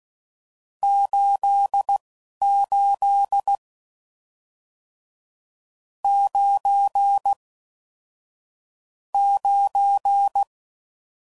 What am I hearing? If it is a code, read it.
Morse code, heard as a keyed tone, about 16 wpm, spaced as for 11 wpm, 800 Hz. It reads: 88 9 9